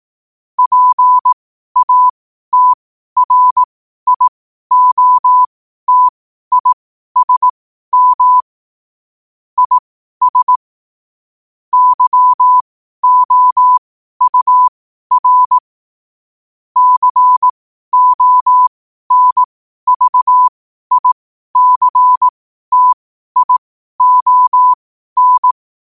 Morse code, heard as a keyed tone, about 18 wpm, spaced as for 12 wpm, 1000 Hz. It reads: PATRIOTISM IS YOUR CONVICTION
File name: qotdfiles / 12wpm